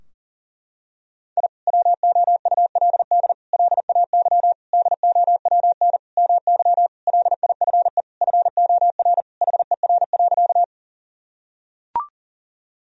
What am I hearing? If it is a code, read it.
Morse code, heard as a keyed tone, about 40 wpm, 700 Hz. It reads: I WOULD LAY DOWN MY LIFE FOR HER.
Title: Sentences from Top 1000 Words - Review Part 2 - 40wpm